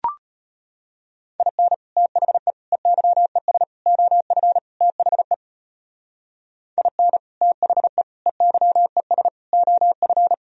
Morse code, heard as a keyed tone, about 29 wpm, 700 Hz. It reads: IN THE EYES OF THE IN THE EYES OF
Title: Speed-Racing - Sets of 5 Words 25wpm